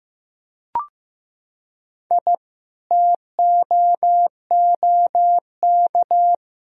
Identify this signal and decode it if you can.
Morse code, heard as a keyed tone, about 15 wpm, 700 Hz. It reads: I TOOK
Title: Sentences from Top 1000 Words - Review Part 4 - 15wpm